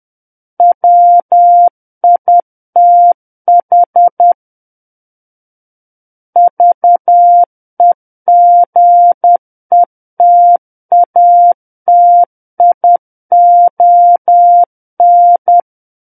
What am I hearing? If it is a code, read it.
Morse code, heard as a keyed tone, about 10 wpm, 700 Hz. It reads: WITH VEGETATION
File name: Book One / 01-Chapter 1 - 10 WPM